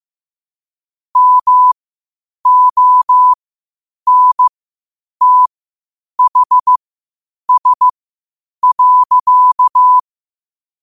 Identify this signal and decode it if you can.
Morse code, heard as a keyed tone, about 15 wpm, spaced as for 8 wpm, 1000 Hz. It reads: MONTHS.
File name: qotdfiles / 8wpm